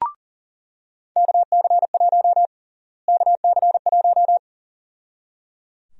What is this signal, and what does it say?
Morse code, heard as a keyed tone, about 37 wpm, 700 Hz. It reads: KC1 KC1
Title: USA Call Sign Prefixes - Repeated 2x - 40wpm